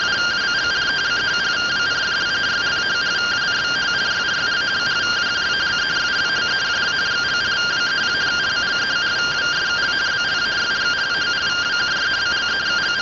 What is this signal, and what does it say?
Two-tone teleprinter signal, about 170 Hz shift.
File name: findings / CT AXIAL -256.gen-31 rtty rx